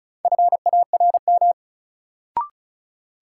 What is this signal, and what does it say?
Morse code, heard as a keyed tone, about 36 wpm, 700 Hz. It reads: FARM
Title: Sentences from Top 1000 Words - Review Part 2 - New Word Drill - 35wpm